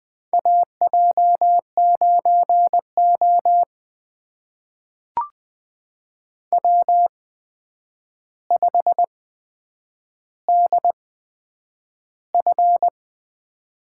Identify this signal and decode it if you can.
Morse code, heard as a keyed tone, about 21 wpm, spaced as for 9 wpm, 700 Hz. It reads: AJ9O W5DF
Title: USA Call Signs - 8x Character Spacing - 20wpm